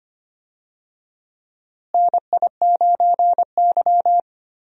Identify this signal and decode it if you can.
Morse code, heard as a keyed tone, about 25 wpm, 700 Hz. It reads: NI9Y
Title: USA Call Signs - 4 Characters - Repeated 3x - 25wpm